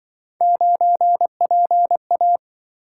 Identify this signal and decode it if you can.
Morse code, heard as a keyed tone, about 26 wpm, 700 Hz. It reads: 9PA